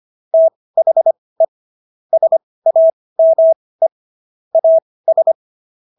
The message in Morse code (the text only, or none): THE SAME AS